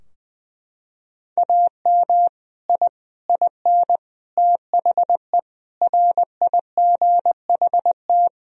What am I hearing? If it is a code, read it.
Morse code, heard as a keyed tone, about 21 wpm, 700 Hz. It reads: AM I IN THE RIGHT
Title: Sentences from Top 1000 Words - Review Part 2 - 20wpm